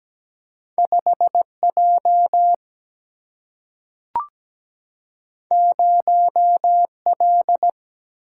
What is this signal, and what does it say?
Morse code, heard as a keyed tone, about 17 wpm, 700 Hz. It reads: 5 J 0 L
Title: Single-Letter Single-Number 17wpm